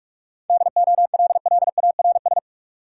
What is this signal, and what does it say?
Morse code, heard as a keyed tone, about 46 wpm, 700 Hz. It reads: DOLLARS
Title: Sets of 5 Words - New Word Drill - 45wpm